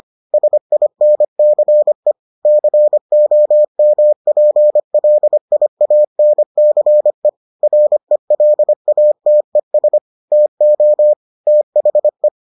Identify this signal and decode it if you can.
Morse code, heard as a keyed tone, about 25 wpm, 600 Hz. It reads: SINCE COMPLIANCE RELATES TO THE